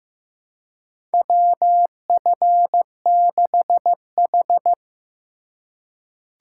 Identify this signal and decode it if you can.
Morse code, heard as a keyed tone, about 15 wpm, 700 Hz. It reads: WF6H
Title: USA Call Signs - 4 Characters - 4x Character Spacing - 15wpm